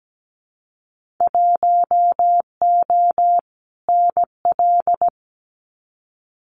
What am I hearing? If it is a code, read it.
Morse code, heard as a keyed tone, about 17 wpm, 700 Hz. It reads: 1O NL
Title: Winter Field Day Exchanges Encore 17wpm